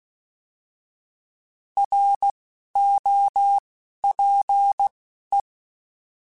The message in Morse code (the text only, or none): R O P E